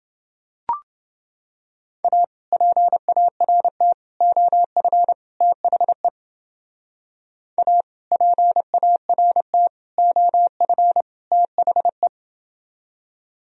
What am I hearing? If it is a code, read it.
Morse code, heard as a keyed tone, about 28 wpm, 700 Hz. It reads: A PART OF THE A PART OF THE